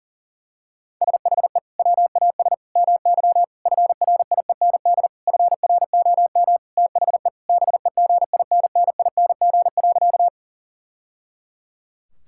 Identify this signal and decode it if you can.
Morse code, heard as a keyed tone, about 40 wpm, 700 Hz. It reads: SHE WAS MY FRIEND FROM THE BEGINNING.
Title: Sentences from Top 1000 Words - Review Part 5 - 40wpm